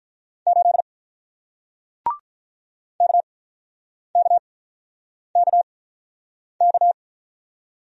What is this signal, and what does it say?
Morse code, heard as a keyed tone, about 48 wpm, spaced as for 9 wpm, 700 Hz. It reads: 8 KKKK